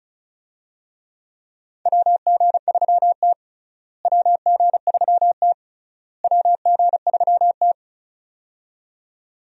Morse code, heard as a keyed tone, about 34 wpm, 700 Hz. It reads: WG3T WG3T WG3T
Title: USA Call Signs - 4 Characters - Repeated 3x - 35wpm